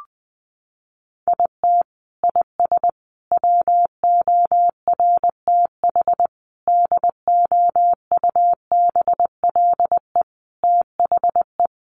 Morse code, heard as a keyed tone, about 20 wpm, 700 Hz. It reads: IT IS WORTH DOUBLE THE
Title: Sentences from Top 1400 Words 20wpm